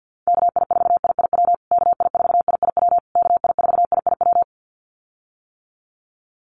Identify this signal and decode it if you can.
Morse code, heard as a keyed tone, about 50 wpm, 700 Hz. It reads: KI4IIW KI4IIW KI4IIW
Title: USA Call Signs Repeated 3x 50wpm